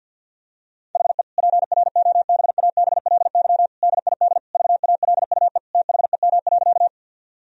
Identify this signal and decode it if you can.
Morse code, heard as a keyed tone, about 50 wpm, 700 Hz. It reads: HE PROBABLY DID VALUE THEM.
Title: Sentences from Top 1000 Words - Review Part 4 - 50wpm